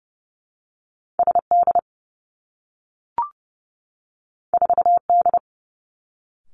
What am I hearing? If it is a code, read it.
Morse code, heard as a keyed tone, about 31 wpm, 700 Hz. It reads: S D 4 D
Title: Two Challenging Letter-Number Combinations 30wpm